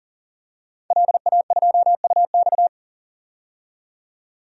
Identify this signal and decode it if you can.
Morse code, heard as a keyed tone, about 40 wpm, 700 Hz. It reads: LA2UX
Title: DXCC-100 Call Signs - 5 Characters - 40wpm